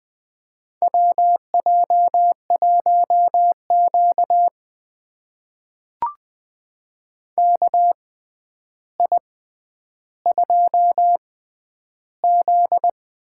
Morse code, heard as a keyed tone, about 20 wpm, spaced as for 10 wpm, 700 Hz. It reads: WJ1Q KI2Z
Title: USA Call Signs - 4 Characters - 6x Character Spacing - 20wpm